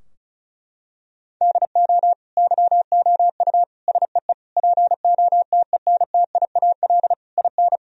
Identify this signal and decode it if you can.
Morse code, heard as a keyed tone, about 36 wpm, 700 Hz. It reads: DO YOU SEE POTENTIAL IN
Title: Sentences from Top 1000 Words - Review Part 4 - 35wpm